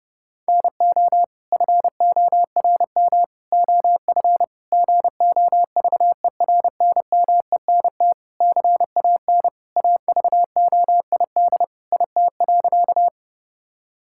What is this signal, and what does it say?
Morse code, heard as a keyed tone, about 28 wpm, 700 Hz. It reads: NO FORM OF GOVERNMENT CAN AVOID IT.
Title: Sentences from Top 1000 Words - Review Part 6 - 30wpm